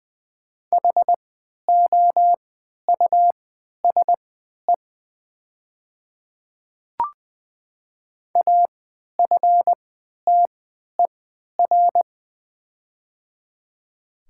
Morse code, heard as a keyed tone, about 20 wpm, spaced as for 11 wpm, 700 Hz. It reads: HOUSE AFTER
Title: 3x Character Spacing - Top 200 Words - 20wpm